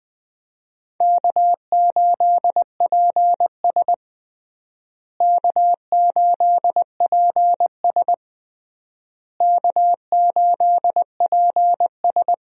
Morse code, heard as a keyed tone, about 20 wpm, 700 Hz. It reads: K8PS K8PS K8PS
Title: USA Call Signs - 4 Characters - Repeated 3x - 20wpm